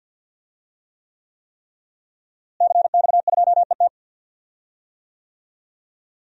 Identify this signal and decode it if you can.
Morse code, heard as a keyed tone, about 48 wpm, 700 Hz. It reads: CX2ET
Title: DXCC-100 Call Signs - 5 Characters - Repeated 3x - 50wpm